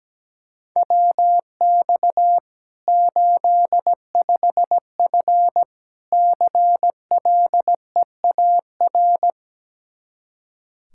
Morse code, heard as a keyed tone, about 17 wpm, 700 Hz. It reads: WX 85F CLEAR
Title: Practice QSO - Exchange 2 - Info - 17wpm